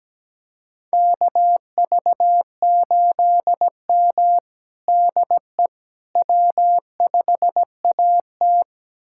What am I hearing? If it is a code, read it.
Morse code, heard as a keyed tone, about 17 wpm, 700 Hz. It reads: KV8M DE W5AT
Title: USA Callsign DE Callsign - 4 Characters - 17wpm